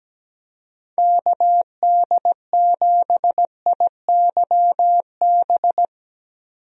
Morse code, heard as a keyed tone, about 17 wpm, 700 Hz. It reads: KD7IYB
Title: USA Call Signs - 6 Characters - 17wpm